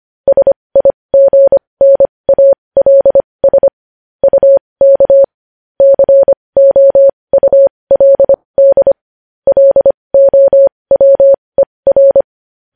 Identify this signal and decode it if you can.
Morse code, heard as a keyed tone, about 25 wpm, spaced as for 20 wpm, 550 Hz. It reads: SIGNALS UK COULD LOWER
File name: podcast / News headlines-1x-25x20